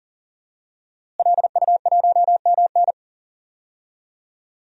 Morse code, heard as a keyed tone, about 40 wpm, 700 Hz. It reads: LU1MN
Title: DXCC-100 Call Signs - 5 Characters - Repeated 2x - 40wpm